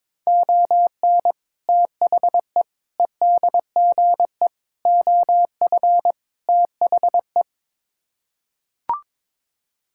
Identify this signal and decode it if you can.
Morse code, heard as a keyed tone, about 21 wpm, 700 Hz. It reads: ON THE EDGE OF THE
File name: practice / 22wpm / Build A Phrase - 5-Word Phrases - 22wpm